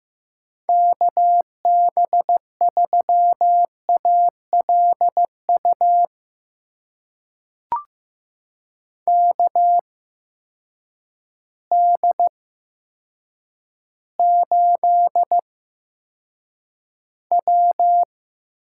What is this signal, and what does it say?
Morse code, heard as a keyed tone, about 15 wpm, spaced as for 8 wpm, 700 Hz. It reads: KB3ALU KD8W
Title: USA Call Signs - 6 Characters - 8x Character Spacing - 15wpm